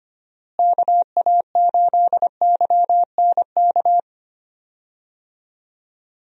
Morse code, heard as a keyed tone, about 25 wpm, 700 Hz. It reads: KA8YNK
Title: USA Call Signs - 6 Characters - Repeated 3x - 25wpm